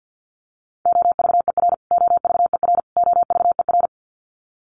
Morse code, heard as a keyed tone, about 50 wpm, 700 Hz. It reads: OVER OVER OVER
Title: Top 300 Words Repeated 50wpm